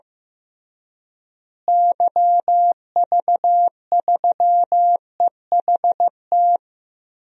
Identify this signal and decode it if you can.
Morse code, heard as a keyed tone, about 16 wpm, 700 Hz. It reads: YV3EHT
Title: DXCC-100 Call Signs - 6 Characters - Repeated 3x - 15wpm